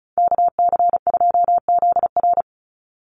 Morse code, heard as a keyed tone, about 34 wpm, 700 Hz. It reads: KC2ZR
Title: USA Call Signs 35wpm